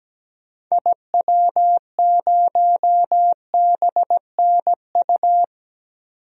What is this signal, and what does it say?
Morse code, heard as a keyed tone, about 17 wpm, 700 Hz. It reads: IW0BNU
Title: DXCC-100 Call Signs - 6 Characters - 17wpm